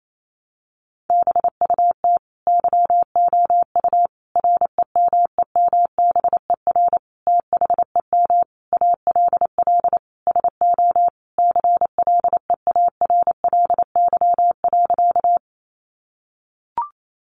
Morse code, heard as a keyed tone, about 28 wpm, 700 Hz. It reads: BUT YOU REMEMBER THEM ALL SO CLEARLY.
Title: Sentences from Top 1300 Words 28wpm